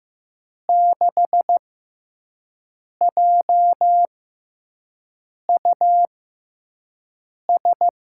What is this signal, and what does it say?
Morse code, heard as a keyed tone, about 15 wpm, spaced as for 5 wpm, 700 Hz. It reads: 6JUS
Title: USA Call Signs - 6 Characters - 6x Character Spacing - 15wpm